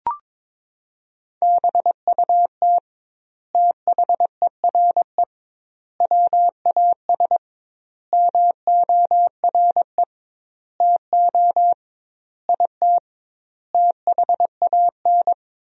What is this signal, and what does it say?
Morse code, heard as a keyed tone, about 21 wpm, 700 Hz. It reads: BUT THERE WAS MORE TO IT THAN
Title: 2x Word-Spacing - Sentences from Top 100 Words - 22wpm